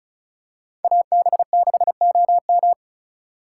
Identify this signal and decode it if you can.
Morse code, heard as a keyed tone, about 34 wpm, 700 Hz. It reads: AB6OM
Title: USA Call Signs - 3x Character Spacing - 35wpm